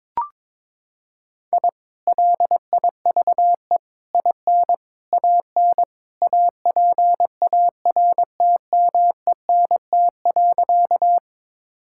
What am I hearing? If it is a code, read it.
Morse code, heard as a keyed tone, about 22 wpm, 700 Hz. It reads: I LIVE IN AN APARTMENT.
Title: Sentences from Top 1000 Words - Review Part 6 - 22wpm